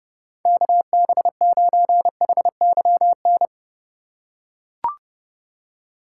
Morse code, heard as a keyed tone, about 30 wpm, 700 Hz. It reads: KB9HYN
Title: USA Call Signs - 2x Character Spacing - 30wpm